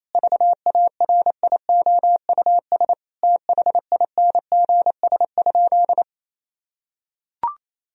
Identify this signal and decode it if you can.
Morse code, heard as a keyed tone, about 28 wpm, 700 Hz. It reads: VARIOUS THINGS?
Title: Sentences from Top 1000 Words - Review Part 4 - 28wpm